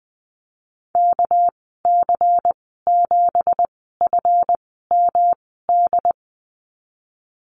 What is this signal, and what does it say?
Morse code, heard as a keyed tone, about 20 wpm, spaced as for 15 wpm, 700 Hz. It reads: KC7FMD
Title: 219 Lesson 35g - Number 0 - Call Signs 20wpm